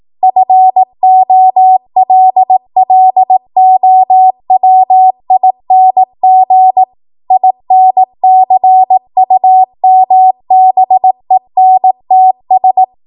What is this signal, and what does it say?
Morse code, heard as a keyed tone, about 18 wpm, 750 Hz. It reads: FOLLOWING INCUMBENTS